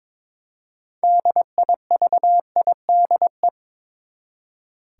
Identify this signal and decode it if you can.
Morse code, heard as a keyed tone, about 22 wpm, 700 Hz. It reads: DIVIDE
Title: Six Letter Words - 22wpm